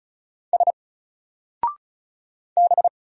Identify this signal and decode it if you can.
Morse code, heard as a keyed tone, about 35 wpm, 700 Hz. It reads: S B
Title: Single Character (Letter:Number:Punctuation:Prosign) 35wpm